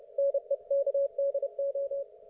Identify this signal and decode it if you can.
Morse code, heard as a keyed tone, about 30 wpm, 560 Hz. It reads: NEKDO
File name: Archiv / CUC208 / ok1fhi cul208